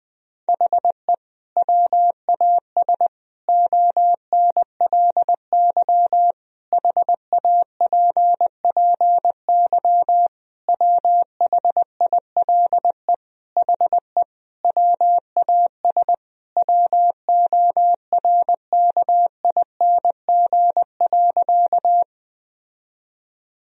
Morse code, heard as a keyed tone, about 21 wpm, 700 Hz. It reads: HE WAS ONLY HAPPY WHILE HE WAS WORKING.
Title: Sentences from Top 1000 Words - Review Part 4 - 20wpm